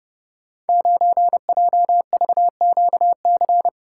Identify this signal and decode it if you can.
Morse code, heard as a keyed tone, about 31 wpm, 700 Hz. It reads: 9JVQC
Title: Code Groups - 5 Letter:Numbers - 30wpm